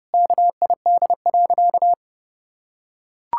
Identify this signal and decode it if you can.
Morse code, heard as a keyed tone, about 30 wpm, 700 Hz. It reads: KID.
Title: Sentences from Top 1600 Words 30wpm